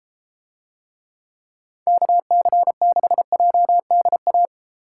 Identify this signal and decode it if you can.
Morse code, heard as a keyed tone, about 32 wpm, 700 Hz. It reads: KC6JDA